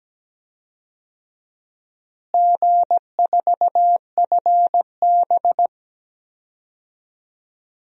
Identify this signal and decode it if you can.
Morse code, heard as a keyed tone, about 17 wpm, 700 Hz. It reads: G4FB
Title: DXCC-100 Call Signs - 4 Characters - Repeated 3x - 17wpm